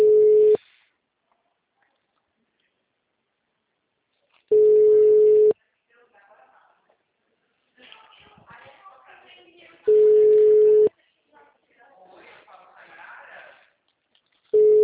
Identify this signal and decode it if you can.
Ringback tone (Europe, 425 Hz), with speech.